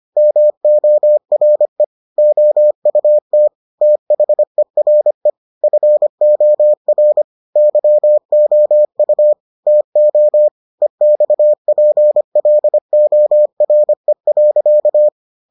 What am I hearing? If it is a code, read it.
Morse code, heard as a keyed tone, about 25 wpm, 600 Hz. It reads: MORE OUT THERE FOR YOU TO EXPLORE.